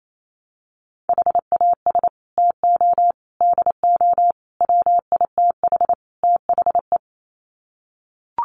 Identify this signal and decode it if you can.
Morse code, heard as a keyed tone, about 28 wpm, 700 Hz. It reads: HAS TO DO WITH THE
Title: Sets of 5 Words 28wpm